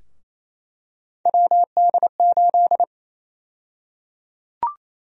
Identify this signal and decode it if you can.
Morse code, heard as a keyed tone, about 28 wpm, 700 Hz. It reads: WD8
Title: USA Call Sign Prefixes - Repeated 2x - 28wpm